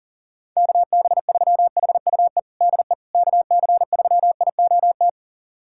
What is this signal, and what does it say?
Morse code, heard as a keyed tone, about 40 wpm, 700 Hz. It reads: KB3HUE DE KC3IOT
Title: USA Callsign DE Callsign - 6 Characters - 40wpm